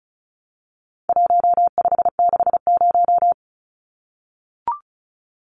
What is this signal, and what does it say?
Morse code, heard as a keyed tone, about 34 wpm, 700 Hz. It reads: 1560
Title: Numbers - Contest Style - 4-Digit 35wpm